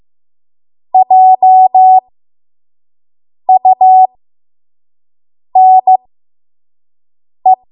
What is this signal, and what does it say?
Morse code, heard as a keyed tone, about 15 wpm, spaced as for 5 wpm, 750 Hz. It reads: JUNE
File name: Morse / 05WPM